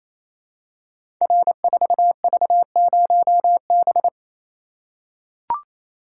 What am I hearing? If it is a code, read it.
Morse code, heard as a keyed tone, about 28 wpm, 700 Hz. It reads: R4V0B
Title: Code Groups - 5 Letter:Numbers - 28wpm